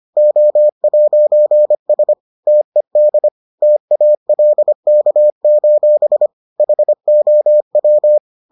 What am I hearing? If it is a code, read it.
Morse code, heard as a keyed tone, about 25 wpm, 600 Hz. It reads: O'S TED TALK: HOW